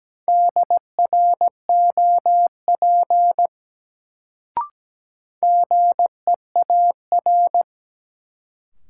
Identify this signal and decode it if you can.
Morse code, heard as a keyed tone, about 17 wpm, 700 Hz. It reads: DROP GEAR
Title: Sentences from Top 1400 Words - New Word Drill Alt - 17wpm